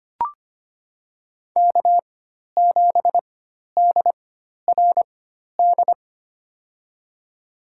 Morse code, heard as a keyed tone, about 25 wpm, spaced as for 12 wpm, 700 Hz. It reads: K7DRD
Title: USA Call Signs - 5 Characters - 4x Character Spacing - 25wpm